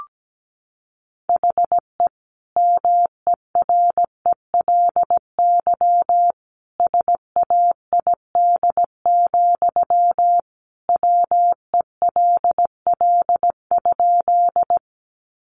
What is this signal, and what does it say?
Morse code, heard as a keyed tone, about 17 wpm, 700 Hz. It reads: HE MERELY SAID, WELL?
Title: Sentences from Top 1400 Words 17wpm